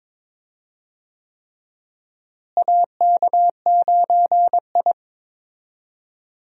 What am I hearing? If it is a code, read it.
Morse code, heard as a keyed tone, about 22 wpm, 700 Hz. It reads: AK9I
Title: USA Call Signs - 4 Characters - Repeated 3x - 22wpm